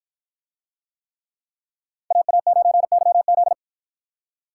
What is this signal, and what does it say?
Morse code, heard as a keyed tone, about 53 wpm, 700 Hz. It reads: AA9YZ